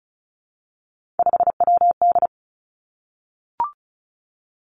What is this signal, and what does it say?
Morse code, heard as a keyed tone, about 36 wpm, 700 Hz. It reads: 5WD
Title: Three Challenging Letter-Number Combinations 35wpm